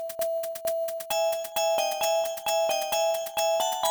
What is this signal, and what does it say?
There is no code.